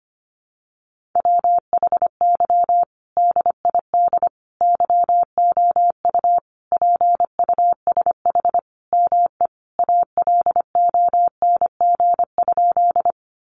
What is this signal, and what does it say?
Morse code, heard as a keyed tone, about 25 wpm, 700 Hz. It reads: WHY DID YOU PUSH ME ALONG?
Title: Sentences from Top 1300 Words 25wpm